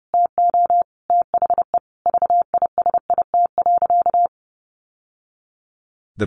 Morse code, heard as a keyed tone, about 29 wpm, 700 Hz. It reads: TO THE VISIT.
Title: Sentences from Top 1200 Words 30wpm